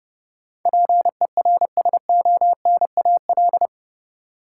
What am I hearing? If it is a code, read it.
Morse code, heard as a keyed tone, about 31 wpm, 700 Hz. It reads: PERSONAL
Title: Sentences from Top 700 Words - New Word Drill - 30wpm